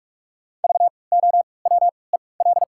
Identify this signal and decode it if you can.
Morse code, heard as a keyed tone, about 46 wpm, spaced as for 26 wpm, 700 Hz. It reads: VOWEL
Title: 3x Character Spacing - Top 500 Words - 45wpm